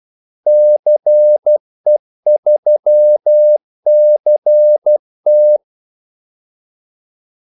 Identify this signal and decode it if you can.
Morse code, heard as a keyed tone, about 12 wpm, 600 Hz. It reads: CE3CT
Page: 2 ekstra ordmellemrum mellem kaldesignalerne